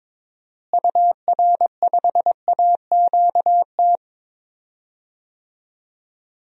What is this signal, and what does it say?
Morse code, heard as a keyed tone, about 22 wpm, 700 Hz. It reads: UR5AQT
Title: DXCC-100 Call Signs - 6 Characters - Repeated 2x - 22wpm